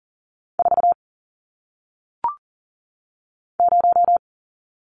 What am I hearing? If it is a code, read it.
Morse code, heard as a keyed tone, about 39 wpm, 700 Hz. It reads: 4 0
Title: Numbers - 1-Digit 40wpm